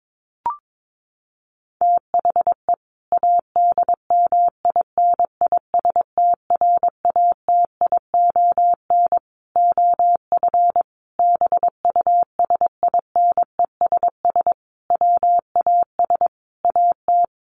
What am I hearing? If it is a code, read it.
Morse code, heard as a keyed tone, about 22 wpm, 700 Hz. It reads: THE ADMINISTRATION OF BUSINESS WAS AT
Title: Sentences from Top 1300 Words 22wpm